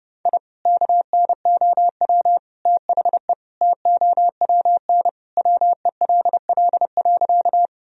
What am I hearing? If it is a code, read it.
Morse code, heard as a keyed tone, about 30 wpm, 700 Hz. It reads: I KNOW THE TOWN WELL.
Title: Sentences from Top 1000 Words - Review Part 2 - 30wpm